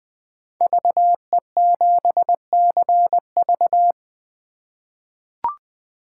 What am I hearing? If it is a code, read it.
Morse code, heard as a keyed tone, about 20 wpm, 700 Hz. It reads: VE7CV
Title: 2023 SOTA Chaser Call Signs - 20wpm